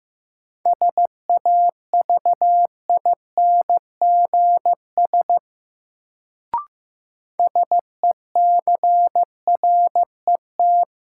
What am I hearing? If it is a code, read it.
Morse code, heard as a keyed tone, about 15 wpm, 700 Hz. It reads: SAVINGS SECRET
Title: Sentences from Top 1200 Words - New Word Drill Alt - 15wpm